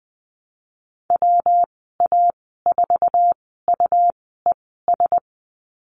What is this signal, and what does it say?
Morse code, heard as a keyed tone, about 20 wpm, spaced as for 14 wpm, 700 Hz. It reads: WA4UES
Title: 128 Lesson 21g - Number 6 - Call Signs 20wpm